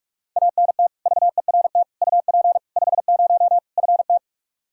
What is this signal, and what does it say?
Morse code, heard as a keyed tone, about 46 wpm, 700 Hz. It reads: ANT VERT UP 50 FT
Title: Practice QSO - Exchange 2 - Info - 45wpm